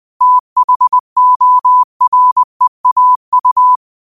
Morse code, heard as a keyed tone, about 20 wpm, 1000 Hz. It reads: THOREAU
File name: qotdfiles / 20wpm